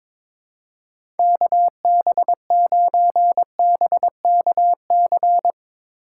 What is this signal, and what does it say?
Morse code, heard as a keyed tone, about 22 wpm, 700 Hz. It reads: KB9BKC
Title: USA Call Signs - 2x Character Spacing - 22wpm